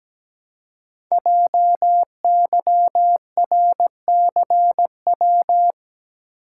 Morse code, heard as a keyed tone, about 17 wpm, 700 Hz. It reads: JYRCW
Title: Code Groups - 5 Letter:Numbers - 17wpm